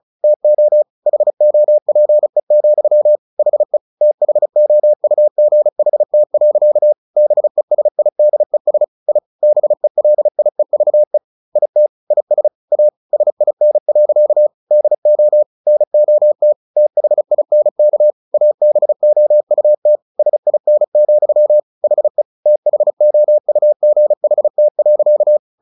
Morse code, heard as a keyed tone, about 34 wpm, 600 Hz. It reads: TO HOPE, HE THOUGHT. BESIDES I BELIEVE IT IS A SIN. DO NOT THINK ABOUT SIN, HE THOUGHT.
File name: the old man and the sea 2 35wpm